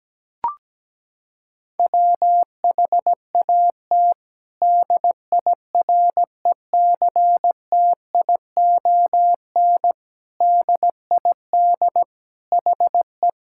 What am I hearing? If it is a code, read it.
Morse code, heard as a keyed tone, about 17 wpm, 700 Hz. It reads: WHAT DIRECTION DID HE
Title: Sentences from Top 1000 Words - Review Part 5 - 17wpm